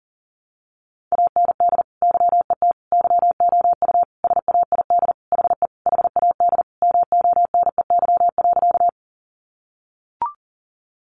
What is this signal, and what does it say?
Morse code, heard as a keyed tone, about 37 wpm, 700 Hz. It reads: AND YET YOU SAID HE HAD MONEY.
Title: Sentences from Top 500 Words 40wpm